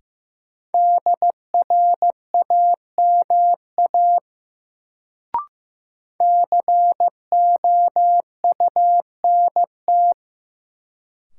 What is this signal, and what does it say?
Morse code, heard as a keyed tone, about 16 wpm, 700 Hz. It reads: DRAMA COUNT
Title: Sentences from Top 1500 Words - New Word Drill Alt - 15wpm